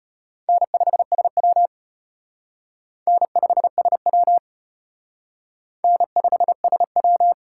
Morse code, heard as a keyed tone, about 33 wpm, 700 Hz. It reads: N5SW N5SW N5SW